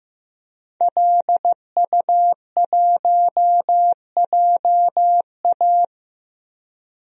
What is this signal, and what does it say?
Morse code, heard as a keyed tone, about 15 wpm, 700 Hz. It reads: LU1JA
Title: DXCC-100 Call Signs - 5 Characters - Repeated 3x - 15wpm